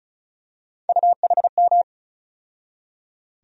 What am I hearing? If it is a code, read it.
Morse code, heard as a keyed tone, about 36 wpm, 700 Hz. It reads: UHM
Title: USA Call Sign Postfixes - 3 Letters 35wpm